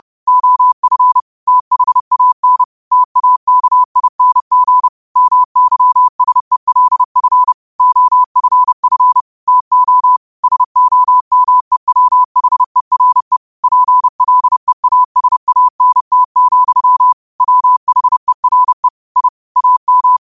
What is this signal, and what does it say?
Morse code, heard as a keyed tone, about 30 wpm, 1000 Hz. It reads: OF THAN TAKING MYSELF OFF TO SOMEWHERE PLEASANT, WHERE I AM